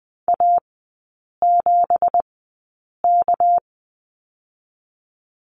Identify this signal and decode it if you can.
Morse code, heard as a keyed tone, about 20 wpm, 700 Hz. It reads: A 7 K
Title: 198 Lesson 32e - Letter k - Three Characters 20wpm